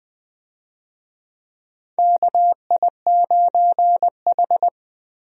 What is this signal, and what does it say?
Morse code, heard as a keyed tone, about 20 wpm, 700 Hz. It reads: KI9H
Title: USA Call Signs - 4 Characters - Repeated 2x - 20wpm